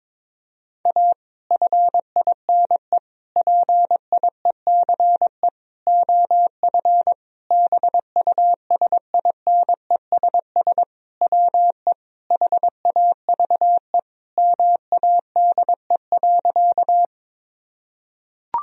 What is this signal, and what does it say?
Morse code, heard as a keyed tone, about 22 wpm, 700 Hz. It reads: A FINE PIECE OF BUSINESS WE HAVE MADE.
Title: Sentences from Top 1000 Words - Review Part 2 - 22wpm